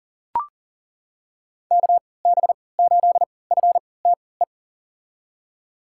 Morse code, heard as a keyed tone, about 38 wpm, spaced as for 22 wpm, 700 Hz. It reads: KB8FTE
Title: USA Call Signs - 6 Characters - 3x Character Spacing - 40wpm